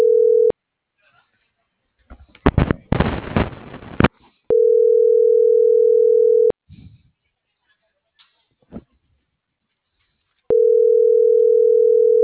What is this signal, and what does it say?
Ringback tone (North America), 440 + 480 Hz.
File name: CENTRAL GOMSA / out francisco.cobos